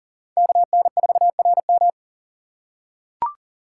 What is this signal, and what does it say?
Morse code, heard as a keyed tone, about 40 wpm, 700 Hz. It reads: KN4RM
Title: USA Call Signs - 5 Characters - 4x Character Spacing - 40wpm